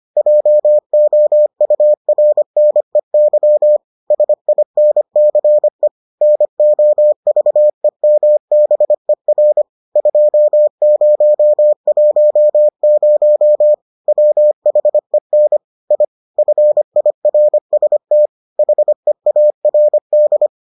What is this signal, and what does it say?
Morse code, heard as a keyed tone, about 25 wpm, 600 Hz. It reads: JOURNEY SINCE NOVEMBER 2010 WHEN I FIRST HEARD